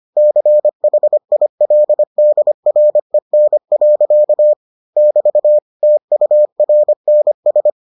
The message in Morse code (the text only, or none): CHILDREN. <BT> TURNS